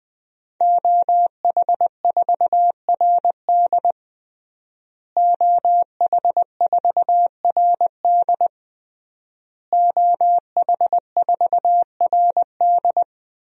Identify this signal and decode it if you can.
Morse code, heard as a keyed tone, about 20 wpm, 700 Hz. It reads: OH4RD OH4RD OH4RD
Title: DXCC-100 Call Signs - 5 Characters - Repeated 3x - 20wpm